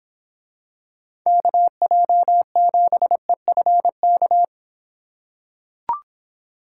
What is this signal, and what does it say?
Morse code, heard as a keyed tone, about 24 wpm, 700 Hz. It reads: KJ7EFK